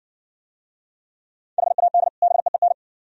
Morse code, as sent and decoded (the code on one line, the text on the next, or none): .... .- -.. / -... . -.
HAD BEEN